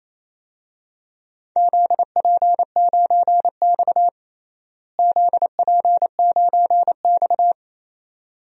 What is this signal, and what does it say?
Morse code, heard as a keyed tone, about 28 wpm, 700 Hz. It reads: ZP9X ZP9X
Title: DXCC-100 Call Signs - 4 Characters - Repeated 2x - 28wpm